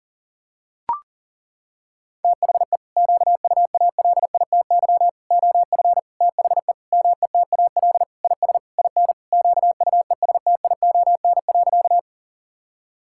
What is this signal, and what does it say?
Morse code, heard as a keyed tone, about 40 wpm, 700 Hz. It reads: THE QUALITY OF THE METAL IS IN QUESTION.
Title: Sentences from Top 1000 Words - Review Part 6 - 40wpm